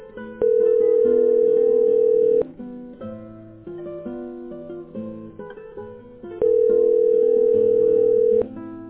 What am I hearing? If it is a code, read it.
Ringback tone (North America), 440 + 480 Hz.